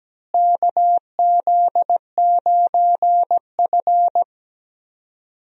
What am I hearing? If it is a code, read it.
Morse code, heard as a keyed tone, about 17 wpm, 700 Hz. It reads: KZ9F
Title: USA Call Signs - 4 Characters - Repeated 2x - 17wpm